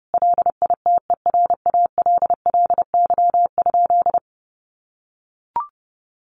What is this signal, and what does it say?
Morse code, heard as a keyed tone, about 28 wpm, 700 Hz. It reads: LITERALLY?
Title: Sentences from Top 1400 Words 30wpm